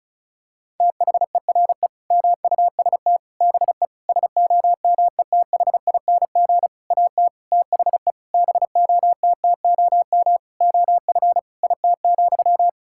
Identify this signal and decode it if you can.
Morse code, heard as a keyed tone, about 34 wpm, 700 Hz. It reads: THERE MUST BE SOMETHING AT THE BOTTOM OF IT,
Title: Sentences from Top 1000 Words - Review Part 6 - 35wpm